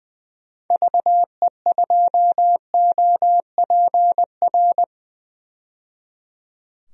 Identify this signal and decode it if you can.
Morse code, heard as a keyed tone, about 20 wpm, 700 Hz. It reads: VE2OPR
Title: DXCC-100 Call Signs - 6 Characters - 20wpm